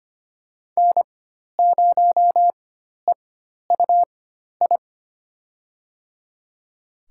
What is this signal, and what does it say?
Morse code, heard as a keyed tone, about 25 wpm, spaced as for 12 wpm, 700 Hz. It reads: N0EUI
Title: USA Call Signs - 5 Characters - 4x Character Spacing - 25wpm